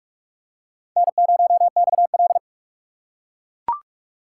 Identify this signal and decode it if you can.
Morse code, heard as a keyed tone, about 43 wpm, 700 Hz. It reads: N0XL